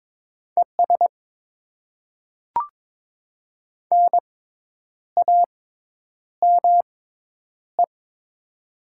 Morse code, heard as a keyed tone, about 22 wpm, spaced as for 9 wpm, 700 Hz. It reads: ES NAME
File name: Top 90 QSO Elements - 6x Character Spacing - 22wpm